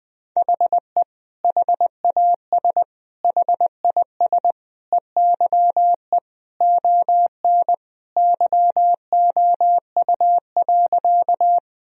Morse code, heard as a keyed tone, about 21 wpm, 700 Hz. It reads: HE HAS HIS EYE ON YOU.
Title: Sentences from Top 1000 Words - Review Part 1 - 20wpm